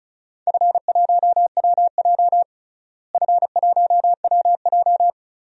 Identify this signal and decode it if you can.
Morse code, heard as a keyed tone, about 36 wpm, 690 Hz. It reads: F1WJ F1WJ
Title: DXCC-100 Call Signs - 4 Characters - Repeated 3x - 35wpm